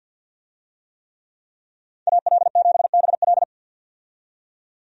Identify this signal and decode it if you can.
Morse code, heard as a keyed tone, about 47 wpm, 700 Hz. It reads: AL7BL
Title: USA Call Signs - 5 Characters - 50wpm